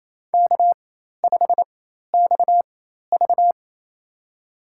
Morse code, heard as a keyed tone, about 28 wpm, spaced as for 13 wpm, 700 Hz. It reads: K5XV
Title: USA Call Signs - 4x Character Spacing - 28wpm